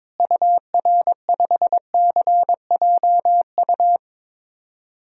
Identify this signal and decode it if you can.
Morse code, heard as a keyed tone, about 22 wpm, 700 Hz. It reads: UR5CJU
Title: DXCC-100 Call Signs - 6 Characters - Repeated 2x - 22wpm